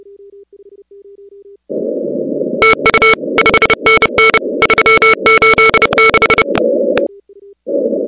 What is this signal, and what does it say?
Morse code, heard as a keyed tone, about 30 wpm, 2200 Hz, with noise. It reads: TU 5NN 386
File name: contests / 2019-cw-wpx-kt5j-k5pi- -2330 0-16